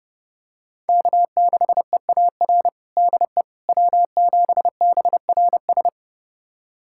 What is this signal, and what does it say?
Morse code, heard as a keyed tone, about 32 wpm, 700 Hz. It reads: K6EAR DE W7BRS
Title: USA Callsign DE Callsign - 5 Characters - 30wpm